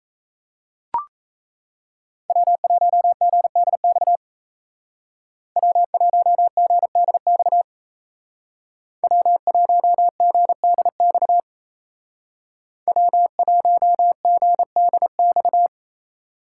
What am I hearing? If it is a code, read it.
Morse code, heard as a keyed tone, about 35 wpm, 700 Hz. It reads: W1GDX W1GDX W1GDX W1GDX